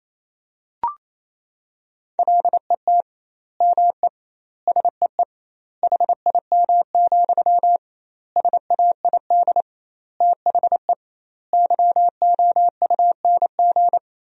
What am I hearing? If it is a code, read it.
Morse code, heard as a keyed tone, about 28 wpm, 700 Hz. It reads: LET ME SEE HIM, SAID THE YOUNG
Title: 2x Word-Spacing - Sentences from Top 400 Words - 28wpm